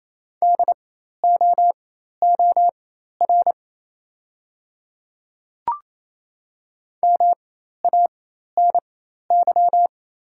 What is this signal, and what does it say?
Morse code, heard as a keyed tone, about 28 wpm, spaced as for 13 wpm, 700 Hz. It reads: DOOR MANY
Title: 4x Character Spacing - Top 300 Words - 28wpm